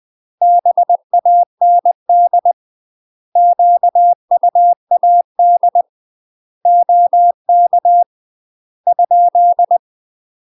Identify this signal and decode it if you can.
Morse code, heard as a keyed tone, about 20 wpm, 700 Hz. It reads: BAND QUAD OK ?
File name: UK MASTER QSOs TRAINING / 20WPM / UK MASTER QSOs 0067